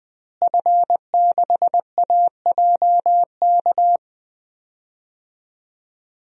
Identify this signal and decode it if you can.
Morse code, heard as a keyed tone, about 20 wpm, 700 Hz. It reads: F6AJK
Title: DXCC-100 Call Signs - 5 Characters - Repeated 3x - 20wpm